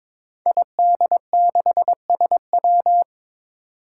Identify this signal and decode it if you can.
Morse code, heard as a keyed tone, about 22 wpm, 700 Hz. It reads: ID6SW
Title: Five Challenging Letter-Number Combinations 22wpm